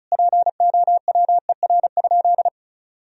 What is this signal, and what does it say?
Morse code, heard as a keyed tone, about 36 wpm, 700 Hz. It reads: POWER?
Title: Sentences from Top 1000 Words - Review Part 2 - 35wpm